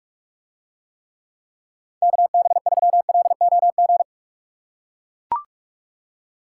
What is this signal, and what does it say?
Morse code, heard as a keyed tone, about 44 wpm, 700 Hz. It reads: KB3LOG